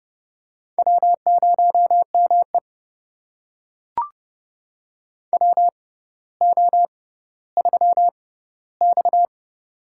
Morse code, heard as a keyed tone, about 30 wpm, spaced as for 15 wpm, 690 Hz. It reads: W0ME WO3X
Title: USA Call Signs - 4 Characters - 6x Character Spacing - 30wpm